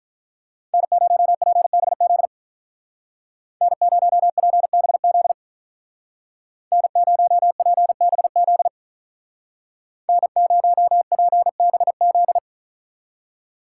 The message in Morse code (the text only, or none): N0PBZ N0PBZ N0PBZ N0PBZ